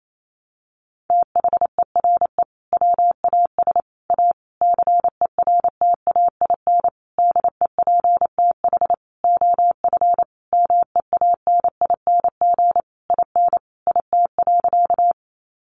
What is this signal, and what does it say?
Morse code, heard as a keyed tone, about 28 wpm, 700 Hz. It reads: THERE WAS A CERTAIN DEPTH OF MEANING IN IT.
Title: Sentences from Top 1200 Words 28wpm